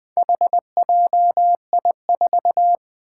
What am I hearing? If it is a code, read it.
Morse code, heard as a keyed tone, about 20 wpm, 700 Hz. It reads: HJI4
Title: Four Challenging Letter-Number Combinations 20wpm